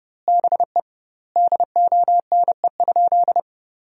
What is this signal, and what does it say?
Morse code, heard as a keyed tone, about 30 wpm, 700 Hz. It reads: BE DONE?
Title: 2x Word-Spacing - Sentences from Top 500 Words - 30wpm